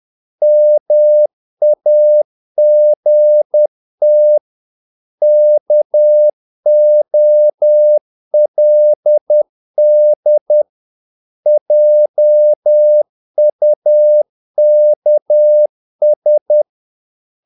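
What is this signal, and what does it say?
Morse code, heard as a keyed tone, about 10 wpm, 600 Hz. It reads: MAGT KOLD JUKS
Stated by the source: Morse practice of short Danish words at 10 wpm, from CW med Gnister.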